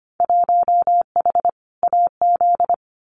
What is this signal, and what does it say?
Morse code, heard as a keyed tone, about 25 wpm, 700 Hz. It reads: 1H AZ
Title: Winter Field Day Exchanges 25wpm